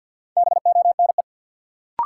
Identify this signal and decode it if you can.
Morse code, heard as a keyed tone, about 47 wpm, 700 Hz. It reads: BONE
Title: Sentences from Top 1600 Words - New Word Drill Alt - 50wpm